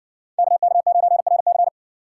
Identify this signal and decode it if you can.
Morse code, heard as a keyed tone, about 59 wpm, 700 Hz. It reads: KK9RC